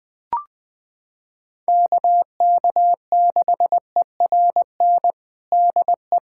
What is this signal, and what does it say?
Morse code, heard as a keyed tone, about 20 wpm, 700 Hz. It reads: KK6ERN DE
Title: USA Callsign DE Callsign - 6 Characters - 20wpm